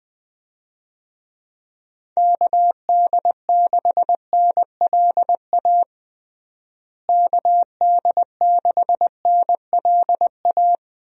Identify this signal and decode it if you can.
Morse code, heard as a keyed tone, about 20 wpm, 700 Hz. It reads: KD6NLA KD6NLA
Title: USA Call Signs - 6 Characters - Repeated 2x - 20wpm